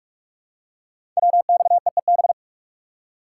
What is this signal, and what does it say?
Morse code, heard as a keyed tone, about 44 wpm, 700 Hz. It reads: WXEEB
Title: Code Groups - 5 Letters - 45wpm